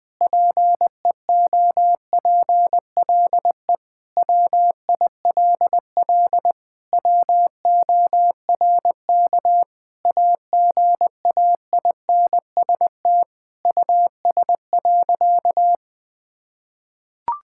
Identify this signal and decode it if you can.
Morse code, heard as a keyed tone, about 20 wpm, 700 Hz. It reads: PEOPLE WILL WORK AGAINST US.
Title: Sentences from Top 1000 Words - Review Part 6 - 20wpm